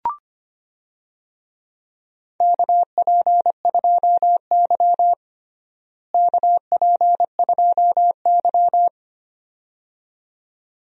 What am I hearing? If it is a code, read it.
Morse code, heard as a keyed tone, about 25 wpm, 700 Hz. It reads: KP2Y KP2Y
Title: DXCC-100 Call Signs - 4 Characters - Repeated 2x - 25wpm